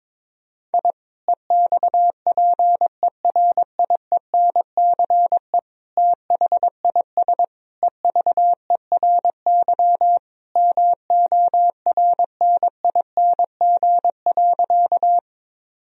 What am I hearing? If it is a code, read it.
Morse code, heard as a keyed tone, about 22 wpm, 700 Hz. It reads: I EXPERIENCE THIS EVERY MORNING.
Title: Sentences from Top 1000 Words - Review Part 3 - 22wpm